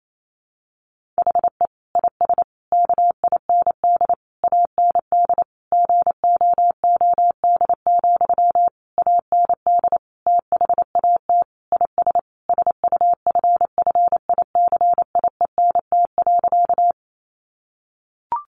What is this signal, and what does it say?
Morse code, heard as a keyed tone, about 28 wpm, 700 Hz. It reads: HE IS KIND AND GOOD, AND THAT IS SUFFICIENT.
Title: Sentences from Top 1400 Words 28wpm